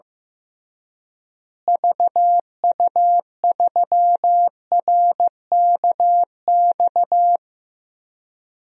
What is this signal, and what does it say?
Morse code, heard as a keyed tone, about 15 wpm, 700 Hz. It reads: VU3RKX
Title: DXCC-100 Call Signs - 6 Characters - Repeated 3x - 15wpm